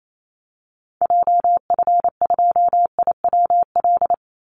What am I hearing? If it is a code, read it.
Morse code, heard as a keyed tone, about 28 wpm, 700 Hz. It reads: JF2IWL
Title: Call Signs Repeated 2x - 28wpm